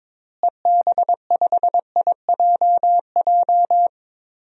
Morse code, heard as a keyed tone, about 22 wpm, 700 Hz. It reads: EB5IJJ
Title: DXCC-100 Call Signs - 6 Characters - Repeated 2x - 22wpm